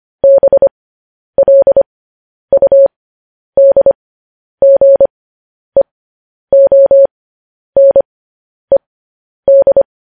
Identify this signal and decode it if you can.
Morse code, heard as a keyed tone, about 25 wpm, spaced as for 10 wpm, 550 Hz. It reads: BLUDGEONED
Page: news headlines in morse code at 10 wpm